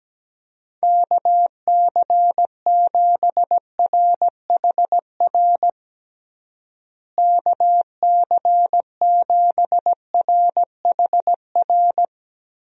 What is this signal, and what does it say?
Morse code, heard as a keyed tone, about 17 wpm, 700 Hz. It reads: KC7RHR KC7RHR
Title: USA Call Signs - 6 Characters - Repeated 3x - 17wpm